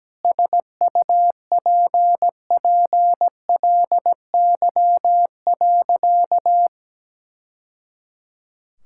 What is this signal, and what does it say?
Morse code, heard as a keyed tone, about 17 wpm, 700 Hz. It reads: SUPPLY.
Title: Sentences from Top 1000 Words - Review Part 5 - 17wpm